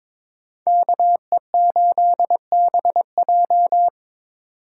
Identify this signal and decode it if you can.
Morse code, heard as a keyed tone, about 26 wpm, 700 Hz. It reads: KE8BJ
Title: Speed Racing - USA 2x2 Call Signs 22wpm